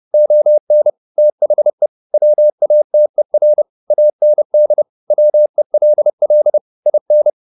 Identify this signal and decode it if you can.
Morse code, heard as a keyed tone, about 30 wpm, 600 Hz. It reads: ON THE WATER AND WELL IN
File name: the old man and the sea / the old man and the sea 1 30wpm